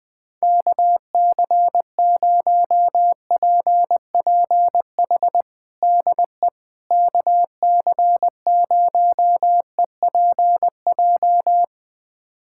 Morse code, heard as a keyed tone, about 20 wpm, 700 Hz. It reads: KC0PPH DE KC0EPJ
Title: USA Callsign DE Callsign 20wpm